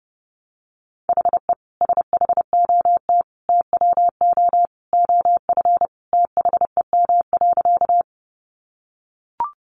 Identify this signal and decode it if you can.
Morse code, heard as a keyed tone, about 30 wpm, 700 Hz. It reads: HE SHOT TWO OF THEM.
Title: Sentences from Top 1100 Words 30wpm